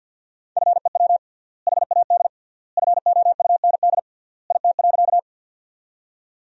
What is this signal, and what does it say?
Morse code, heard as a keyed tone, about 47 wpm, 700 Hz. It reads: FEW HAD FOUND IT.
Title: 3x Word-Spacing - Sentences from Top 300 Words - 50wpm